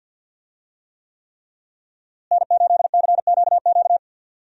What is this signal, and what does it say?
Morse code, heard as a keyed tone, about 52 wpm, 700 Hz. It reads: N8CQQ
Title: USA Call Signs - 5 Characters - Repeated 3x - 50wpm